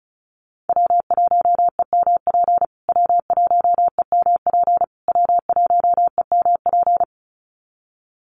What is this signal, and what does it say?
Morse code, heard as a keyed tone, about 34 wpm, 700 Hz. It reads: W1EMP W1EMP W1EMP
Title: USA Call Signs Repeated 3x 35wpm